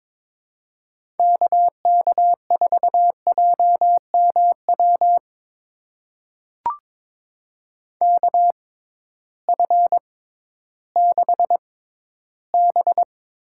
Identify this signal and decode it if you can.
Morse code, heard as a keyed tone, about 22 wpm, spaced as for 13 wpm, 700 Hz. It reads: KK4JMW KF6B
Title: USA Call Signs - 6 Characters - 6x Character Spacing - 22wpm